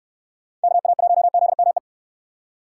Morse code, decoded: KA1ZME

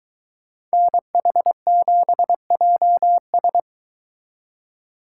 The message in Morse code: NH7JS

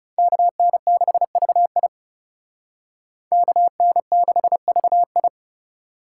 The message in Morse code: KN6VI KN6VI

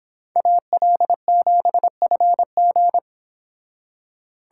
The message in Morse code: AL7FG

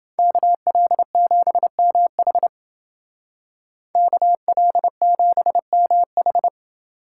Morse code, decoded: KL7MH KL7MH

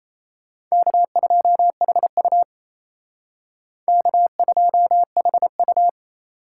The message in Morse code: K2HU K2HU

